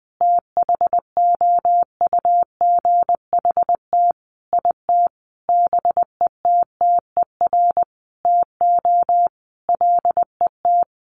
Morse code, heard as a keyed tone, about 20 wpm, 700 Hz. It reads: THOUGHT IT BETTER TO LET
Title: Sentences from Top 1400 Words 20wpm